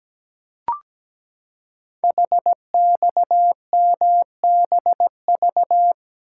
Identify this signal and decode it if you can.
Morse code, heard as a keyed tone, about 17 wpm, 700 Hz. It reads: HXMBV
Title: Code Groups - 5 Letters - 17wpm